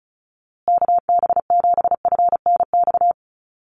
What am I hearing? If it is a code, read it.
Morse code, heard as a keyed tone, about 34 wpm, 700 Hz. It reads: KB7FNX
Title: USA Call Signs Repeated 2x 35wpm